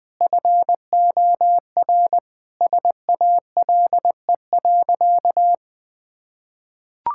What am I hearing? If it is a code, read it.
Morse code, heard as a keyed tone, about 19 wpm, 700 Hz. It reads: FOR SALE.
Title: Sentences from Top 1600 Words 20wpm